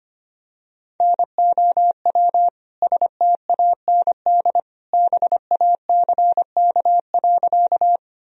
Morse code, heard as a keyed tone, about 25 wpm, 700 Hz. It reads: NOW STAND BACK.
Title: Sentences from Top 1000 Words - Review Part 1 - 25wpm